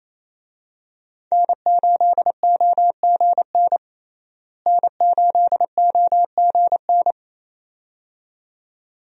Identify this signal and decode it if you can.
Morse code, heard as a keyed tone, about 27 wpm, 700 Hz. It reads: N8OGN N8OGN
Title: USA Call Signs - 5 Characters - Repeated 2x - 28wpm